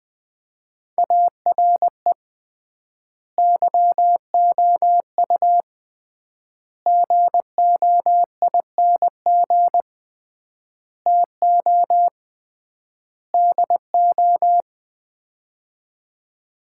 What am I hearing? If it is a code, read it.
Morse code, heard as a keyed tone, about 20 wpm, 700 Hz. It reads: ARE YOU GOING TO DO
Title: 3x Word-Spacing - Sets of 5 Words - 20wpm